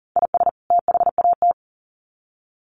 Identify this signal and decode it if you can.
Morse code, heard as a keyed tone, about 40 wpm, 700 Hz. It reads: IS THAT
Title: Sets of 2 Words 40wpm